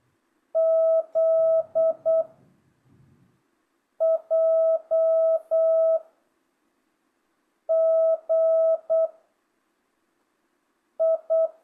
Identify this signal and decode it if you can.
Morse code, heard as a keyed tone, about 8 wpm, 660 Hz. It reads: Z J G I
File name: jh8cbh / 2025cw